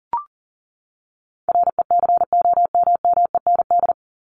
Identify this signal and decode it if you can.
Morse code, heard as a keyed tone, about 40 wpm, 700 Hz. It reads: RECOMMEND
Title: Top 1001-1250 Words 40wpm